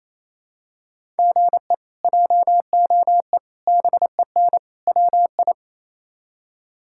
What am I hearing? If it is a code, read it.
Morse code, heard as a keyed tone, about 27 wpm, 700 Hz. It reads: GE JOE BEN WI
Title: K1USN SST QSO Practice - 28wpm